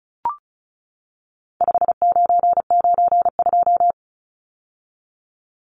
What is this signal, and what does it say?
Morse code, heard as a keyed tone, about 34 wpm, 700 Hz. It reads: 5992
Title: Numbers - 4-Digit 35wpm